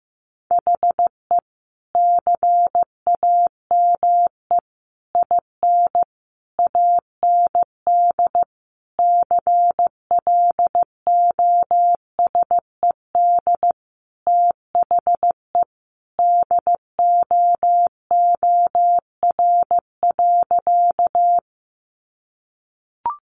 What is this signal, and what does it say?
Morse code, heard as a keyed tone, about 15 wpm, 700 Hz. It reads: HE CAME IN AND CLOSED THE DOOR.
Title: Sentences from Top 1400 Words 15wpm